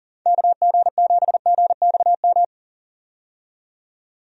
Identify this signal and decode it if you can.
Morse code, heard as a keyed tone, about 40 wpm, 700 Hz. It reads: KG7GXM
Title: USA Call Signs - 6 Characters - 40wpm